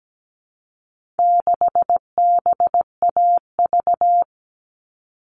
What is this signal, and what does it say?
Morse code, heard as a keyed tone, about 17 wpm, 700 Hz. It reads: 6BAV
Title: All Cut Numbers - 4-Digit 17wpm